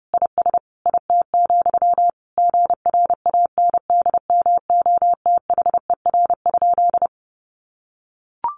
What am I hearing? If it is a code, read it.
Morse code, heard as a keyed tone, about 30 wpm, 700 Hz. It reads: IS IT, GRANDMOTHER?
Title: Sentences from Top 1200 Words 30wpm